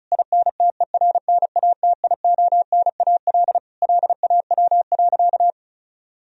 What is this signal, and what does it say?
Morse code, heard as a keyed tone, about 36 wpm, 700 Hz. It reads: INTERNATIONAL LAW.
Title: Sentences from Top 1000 Words - Review Part 4 - 35wpm